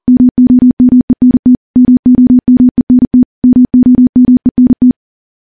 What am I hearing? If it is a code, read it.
Morse code, heard as a keyed tone, about 42 wpm, 250 Hz. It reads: MOMENT MOMENT MOMENT
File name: morse / morse-training-40WPM